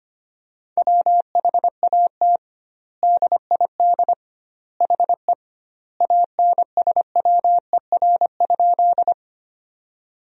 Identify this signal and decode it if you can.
Morse code, heard as a keyed tone, about 25 wpm, 700 Hz. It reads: WHAT DID HE ANSWER?
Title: 2x Word-Spacing - Sentences from Top 300 Words - 25wpm